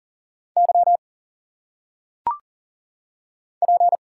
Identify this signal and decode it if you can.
Morse code, heard as a keyed tone, about 40 wpm, 700 Hz. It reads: Y P
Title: Single Letters - Phonetic Pronunciation 40wpm